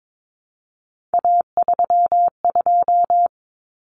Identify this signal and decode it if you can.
Morse code, heard as a keyed tone, about 22 wpm, 700 Hz. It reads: A32
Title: Common Cut Numbers - 3-Digit 22wpm